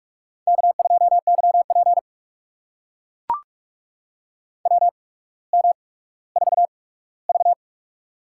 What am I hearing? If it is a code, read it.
Morse code, heard as a keyed tone, about 44 wpm, spaced as for 19 wpm, 700 Hz. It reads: K2YP WM4V
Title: USA Call Signs - 4 Characters - 8x Character Spacing - 45wpm